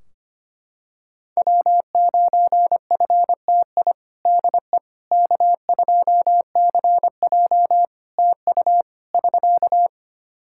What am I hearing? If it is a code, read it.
Morse code, heard as a keyed tone, about 25 wpm, 700 Hz. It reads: W9FTI DE K2CJ TU <SK>
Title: K1USN SST QSO Practice 25wpm